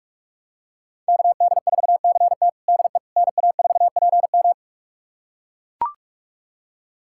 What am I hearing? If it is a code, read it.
Morse code, heard as a keyed tone, about 42 wpm, 700 Hz. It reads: KD4CT DE NA4PM
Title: USA Callsign DE Callsign - 5 Characters - 45wpm